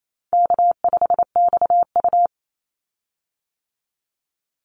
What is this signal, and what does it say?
Morse code, heard as a keyed tone, about 28 wpm, 700 Hz. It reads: K5XU
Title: Call Signs - 28wpm